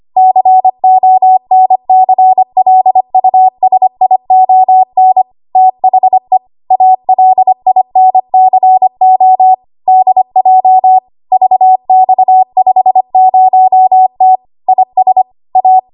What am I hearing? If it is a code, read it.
Morse code, heard as a keyed tone, about 25 wpm, 750 Hz. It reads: CONCLUSION THE ALINCO DJ VX50T IS A
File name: Archive / 25 WPM / 25WPM